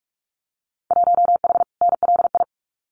Morse code, heard as a keyed tone, about 46 wpm, 700 Hz. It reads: 1H NLI
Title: Winter Field Day Exchanges 45wpm